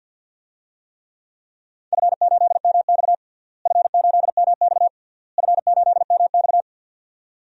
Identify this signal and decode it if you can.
Morse code, heard as a keyed tone, about 51 wpm, 700 Hz. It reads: F8MX F8MX F8MX
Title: DXCC-100 Call Signs - 4 Characters - Repeated 3x - 50wpm